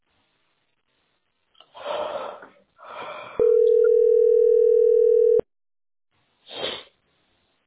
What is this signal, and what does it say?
Ringback tone (North America), 440 + 480 Hz.
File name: 2e -e20f-42ef-81c7